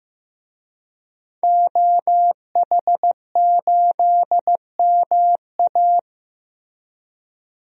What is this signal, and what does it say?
Morse code, heard as a keyed tone, about 15 wpm, 700 Hz. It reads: OH8MA
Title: DXCC-100 Call Signs - 5 Characters - Repeated 3x - 15wpm